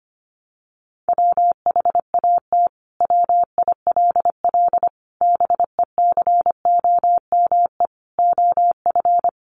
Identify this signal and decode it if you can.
Morse code, heard as a keyed tone, about 25 wpm, 700 Hz. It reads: WHAT WILL BECOME OF
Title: Sentences from Top 600 Words 25wpm